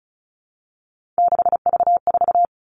Morse code, heard as a keyed tone, about 34 wpm, 700 Hz. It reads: 6V4
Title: All Cut Numbers - 3-Digit 35wpm